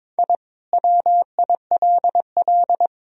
Morse code, heard as a keyed tone, about 22 wpm, 700 Hz. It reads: I WILL
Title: Sentences from Top 1000 Words - Review Part 2 - 22wpm